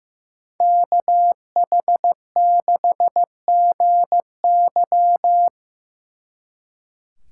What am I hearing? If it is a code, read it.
Morse code, heard as a keyed tone, about 18 wpm, 700 Hz. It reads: KH6GY
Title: Speed Racing - USA 2x2 Call Signs 15wpm